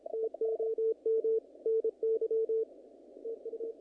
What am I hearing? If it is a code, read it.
Morse code, heard as a keyed tone, about 31 wpm, 450 Hz. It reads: TOM NY TU
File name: contests / naqp-cw-k5tr-op-k5pi- -2000